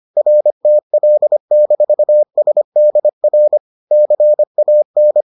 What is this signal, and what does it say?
Morse code, heard as a keyed tone, about 25 wpm, 600 Hz. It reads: RTL-SDR CAN